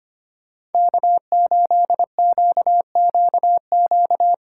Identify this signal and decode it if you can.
Morse code, heard as a keyed tone, about 31 wpm, 700 Hz. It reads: K8QQQ
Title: Speed Racing - USA 1x3 Call Signs 25wpm